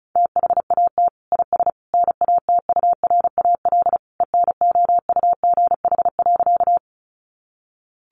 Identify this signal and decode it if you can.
Morse code, heard as a keyed tone, about 34 wpm, 700 Hz. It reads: THAT IS NATURAL ENOUGH.
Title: Sentences from Top 700 Words 35wpm